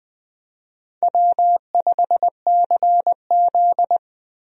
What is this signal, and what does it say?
Morse code, heard as a keyed tone, about 21 wpm, 700 Hz. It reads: W5CZ
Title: USA Call Signs - 4 Characters - Repeated 2x - 20wpm